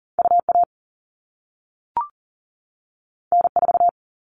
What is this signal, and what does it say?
Morse code, heard as a keyed tone, about 40 wpm, 700 Hz. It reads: UA N4
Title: All Cut Numbers - 2-Digit 40wpm